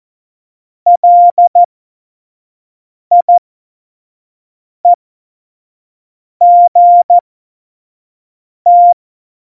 Morse code, heard as a keyed tone, about 14 wpm, spaced as for 5 wpm, 700 Hz. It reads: LIEGT
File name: Morsekurs / Witze / Witz 29. September 2020, 25 F70